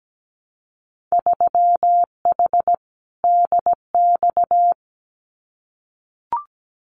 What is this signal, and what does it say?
Morse code, heard as a keyed tone, about 17 wpm, 700 Hz. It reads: 3H DX
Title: Winter Field Day Exchanges 17wpm